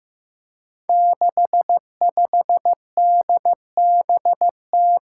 Morse code, heard as a keyed tone, about 15 wpm, 700 Hz. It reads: 65DBT
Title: Five Challenging Letter-Number Combinations 15wpm